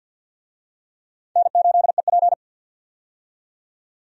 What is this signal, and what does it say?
Morse code, heard as a keyed tone, about 51 wpm, 700 Hz. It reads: N8EP
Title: USA Call Signs - 4 Characters - 50wpm